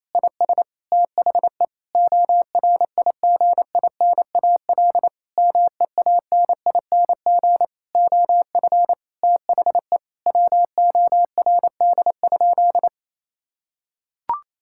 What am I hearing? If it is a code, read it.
Morse code, heard as a keyed tone, about 27 wpm, 700 Hz. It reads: IS THE ORIGINAL MEANING OF THE WORD?
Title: Sentences from Top 1000 Words - Review Part 6 - 28wpm